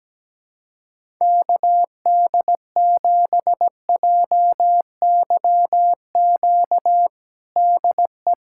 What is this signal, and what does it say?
Morse code, heard as a keyed tone, about 17 wpm, 700 Hz. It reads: KD7JYQ DE
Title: USA Callsign DE Callsign 17wpm